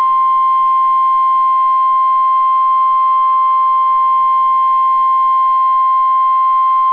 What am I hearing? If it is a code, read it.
Emergency Alert System (SAME) tones (the 1050 Hz attention tone). It is heard from 1.1 to 6.2 s.